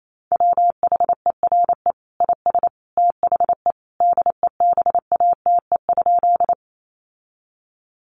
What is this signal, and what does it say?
Morse code, heard as a keyed tone, about 28 wpm, 700 Hz. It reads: WHERE IS THE DEBATE?
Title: Sentences from Top 1100 Words 28wpm